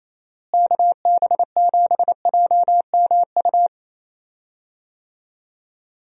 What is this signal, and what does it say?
Morse code, heard as a keyed tone, about 27 wpm, 700 Hz. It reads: KB7JMU
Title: USA Call Signs - 6 Characters - Repeated 3x - 28wpm